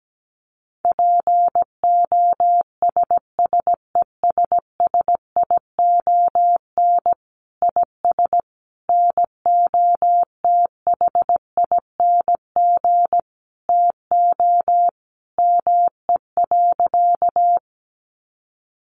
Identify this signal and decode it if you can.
Morse code, heard as a keyed tone, about 17 wpm, 700 Hz. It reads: POSSESSION IS NOTHING TO ME.
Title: Sentences from Top 1400 Words 17wpm